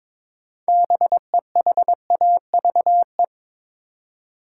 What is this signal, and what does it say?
Morse code, heard as a keyed tone, about 22 wpm, 700 Hz. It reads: BEHAVE
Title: Six Letter Words - 22wpm